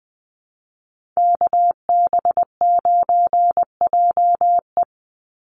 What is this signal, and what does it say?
Morse code, heard as a keyed tone, about 20 wpm, 700 Hz. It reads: KB9JE
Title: USA Call Signs 20wpm